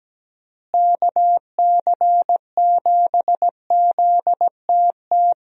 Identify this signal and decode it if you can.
Morse code, heard as a keyed tone, about 17 wpm, 700 Hz. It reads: KC7ZTT
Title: USA Call Signs - 6 Characters - Repeated 3x - 17wpm